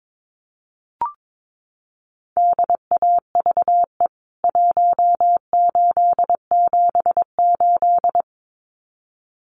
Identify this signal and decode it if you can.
Morse code, heard as a keyed tone, about 22 wpm, 700 Hz. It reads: DAVE 1878
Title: CWT Exchange 22wpm